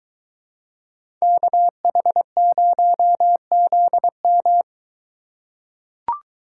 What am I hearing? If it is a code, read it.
Morse code, heard as a keyed tone, about 22 wpm, 700 Hz. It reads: KH0ZM